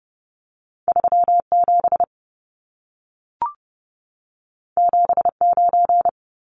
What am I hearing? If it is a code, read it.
Morse code, heard as a keyed tone, about 30 wpm, 700 Hz. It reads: 37 79
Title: Numbers - 2-Digit 30wpm